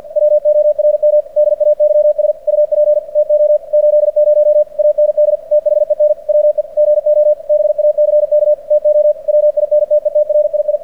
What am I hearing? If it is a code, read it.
Morse code, heard as a keyed tone, about 46 wpm, 600 Hz. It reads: WORM DAYS UP TO 80 AND THEN WE GO BACK TO WINTETRII
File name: archived / CRSnet